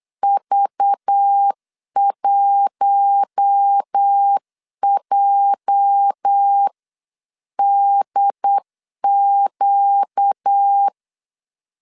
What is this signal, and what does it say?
Morse code, heard as a keyed tone, about 8 wpm, 800 Hz. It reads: V1JDQ
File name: CW / DET / 1 9 1500C S A 8wpm 5637